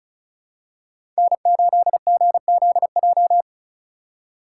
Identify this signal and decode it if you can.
Morse code, heard as a keyed tone, about 34 wpm, 700 Hz. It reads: N8GZJ
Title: USA Call Signs - 2x Character Spacing - 35wpm